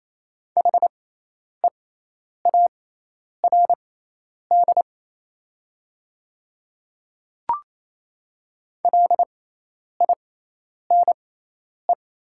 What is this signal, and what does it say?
Morse code, heard as a keyed tone, about 28 wpm, spaced as for 10 wpm, 700 Hz. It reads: HEARD LINE
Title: 6x Character Spacing - Top 500 Words - 28wpm